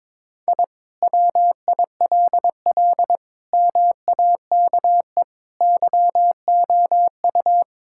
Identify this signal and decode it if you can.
Morse code, heard as a keyed tone, about 22 wpm, 700 Hz. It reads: I WILL MAKE YOU
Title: Sentences from Top 1000 Words - Review Part 5 - 22wpm